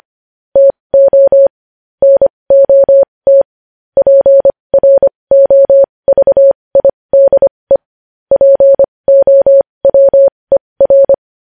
Morse code, heard as a keyed tone, about 25 wpm, spaced as for 20 wpm, 550 Hz. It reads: TO NOT PROVIDE POWER
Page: news headlines in morse code at 20 wpm